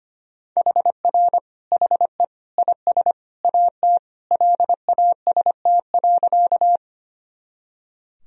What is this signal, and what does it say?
Morse code, heard as a keyed tone, about 25 wpm, 700 Hz. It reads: HR HE IS AT LAST.
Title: Sentences from Top 300 Words and Abbreviations 25wpm